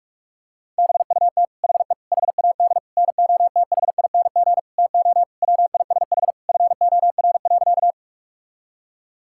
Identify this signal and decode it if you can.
Morse code, heard as a keyed tone, about 46 wpm, 700 Hz. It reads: BUT HE HAD NOTHING TO WISH FOR.
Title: Sentences from Top 1600 Words 45wpm